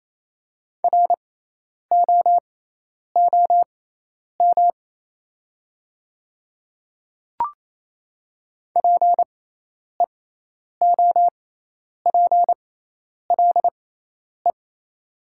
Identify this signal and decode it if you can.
Morse code, heard as a keyed tone, about 28 wpm, spaced as for 10 wpm, 700 Hz. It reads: ROOM PEOPLE
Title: 6x Character Spacing - Top 500 Words - 28wpm